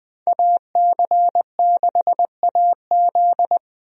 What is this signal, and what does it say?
Morse code, heard as a keyed tone, about 20 wpm, 700 Hz. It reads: AC6AZ
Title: USA Call Signs - 5 Characters - Repeated 3x - 20wpm